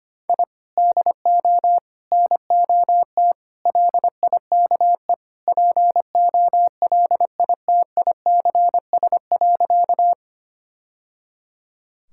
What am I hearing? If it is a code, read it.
Morse code, heard as a keyed tone, about 25 wpm, 700 Hz. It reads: I DO NOT LIKE POLITICS.
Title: Sentences from Top 1000 Words - Review Part 6 - 25wpm